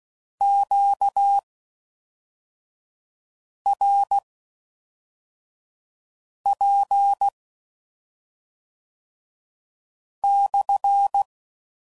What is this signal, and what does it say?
Morse code, heard as a keyed tone, about 16 wpm, spaced as for 3 wpm, 800 Hz. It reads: QRP/